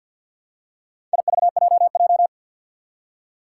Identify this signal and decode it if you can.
Morse code, heard as a keyed tone, about 49 wpm, 700 Hz. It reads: IVJJ
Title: Four Challenging Letter-Number Combinations 50wpm